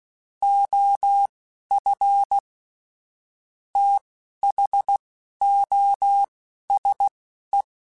Morse code, heard as a keyed tone, about 16 wpm, spaced as for 11 wpm, 800 Hz. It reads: OF THOSE
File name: Technical Subjects / Technical Subjects-mp3s / OR Gate @10